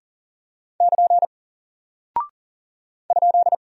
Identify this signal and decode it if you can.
Morse code, heard as a keyed tone, about 40 wpm, 700 Hz. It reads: <KN> ?
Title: Single Character (Letter:Number:Punctuation:Prosign) Alt 40wpm